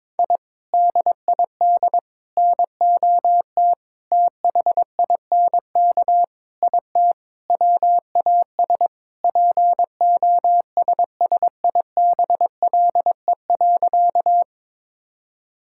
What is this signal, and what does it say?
Morse code, heard as a keyed tone, about 22 wpm, 700 Hz. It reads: I DID NOT THINK IT WAS POSSIBLE.
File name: practice / 22wpm / Sentences with 4-Word Phrases - 22wpm